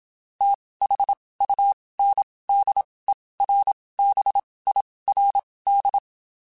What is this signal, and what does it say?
Morse code, heard as a keyed tone, about 27 wpm, spaced as for 19 wpm, 800 Hz. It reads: THUNDERBIRD
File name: (spaced) / QSOs Spaced-mp3s / WD0BGP 1 @25 sp